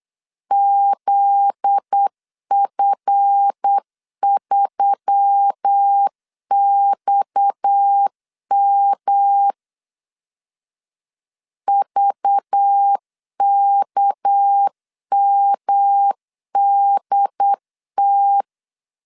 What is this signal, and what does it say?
Morse code, heard as a keyed tone, about 8 wpm, 800 Hz. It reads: ZF3XM VKMDT
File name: LEZIONI / 7 GF9Z / 7 GF9Z 8wpm 3213